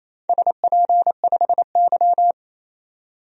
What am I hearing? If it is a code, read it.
Morse code, heard as a keyed tone, about 28 wpm, 700 Hz. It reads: SP5Y
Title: DXCC-100 Call Signs - 4 Characters - 28wpm